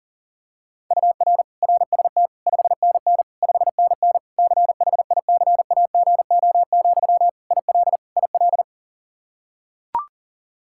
Morse code, heard as a keyed tone, about 40 wpm, 700 Hz. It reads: UR RST 5NN 5NN CHICAGO, IL IL
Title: K1USN SST QSO Practice 40wpm